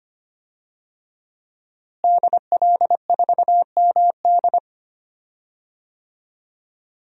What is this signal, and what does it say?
Morse code, heard as a keyed tone, about 25 wpm, 700 Hz. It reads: DL4MD
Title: DXCC-100 Call Signs - 5 Characters - Repeated 3x - 25wpm